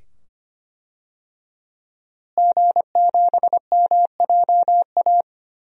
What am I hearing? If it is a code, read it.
Morse code, heard as a keyed tone, about 25 wpm, 700 Hz. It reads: G7MJA
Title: DXCC-100 Call Signs - 5 Characters - Repeated 2x - 25wpm